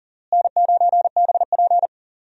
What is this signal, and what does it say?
Morse code, heard as a keyed tone, about 40 wpm, 700 Hz. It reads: N9BP
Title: USA Call Signs - 4 Characters - Repeated 2x - 40wpm